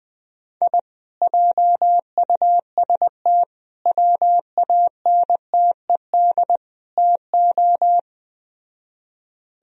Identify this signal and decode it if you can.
Morse code, heard as a keyed tone, about 20 wpm, 700 Hz. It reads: I JUST WANTED TO
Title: Speed-Racing - Sets of 4 Words 17wpm